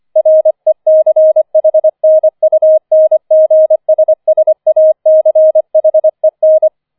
Morse code, heard as a keyed tone, about 25 wpm, 630 Hz. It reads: RECHNUNGSSACHEN